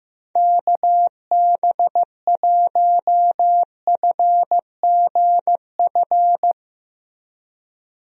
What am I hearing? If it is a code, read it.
Morse code, heard as a keyed tone, about 15 wpm, 700 Hz. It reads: KB1FGF
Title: USA Call Signs - 6 Characters - Repeated 2x - 15wpm